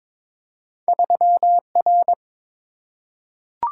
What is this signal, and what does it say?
Morse code, heard as a keyed tone, about 22 wpm, 700 Hz. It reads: 3R
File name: practice / 22wpm / Two Letter-Number Combinations - 22wpm